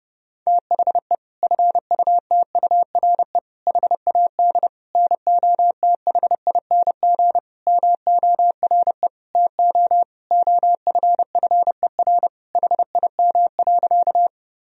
Morse code, heard as a keyed tone, about 30 wpm, 700 Hz. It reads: THE FUTURE HAD NOTHING MORE TO OFFER HIM.
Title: Sentences from Top 1000 Words - Review Part 4 - 30wpm